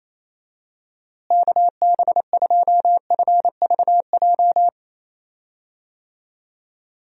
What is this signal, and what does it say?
Morse code, heard as a keyed tone, about 28 wpm, 700 Hz. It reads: KB2FVJ
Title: USA Call Signs - 6 Characters - Repeated 2x - 28wpm